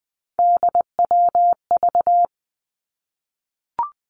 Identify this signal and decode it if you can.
Morse code, heard as a keyed tone, about 19 wpm, 700 Hz. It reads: DWV
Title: Three Challenging Letter-Number Combinations 20wpm